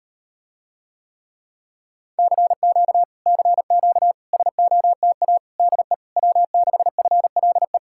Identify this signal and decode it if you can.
Morse code, heard as a keyed tone, about 39 wpm, 700 Hz. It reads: CQ CQ SOTA DE W6FLE